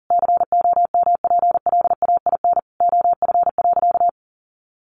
Morse code, heard as a keyed tone, about 40 wpm, 700 Hz. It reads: COMPLAIN OF.
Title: Sentences from Top 1300 Words 40wpm